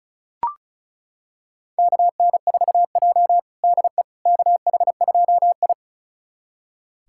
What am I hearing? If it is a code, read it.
Morse code, heard as a keyed tone, about 34 wpm, 700 Hz. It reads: KN4J DE KH2I
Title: USA Callsign DE Callsign - 4 Characters - 35wpm